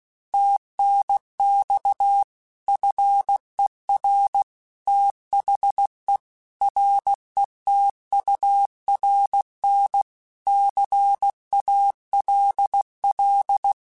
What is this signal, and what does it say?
Morse code, heard as a keyed tone, about 16 wpm, 800 Hz. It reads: TNX FER THE RETURN CALL
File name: QSOs / QSOs-mp3s / WA7QOX 2@15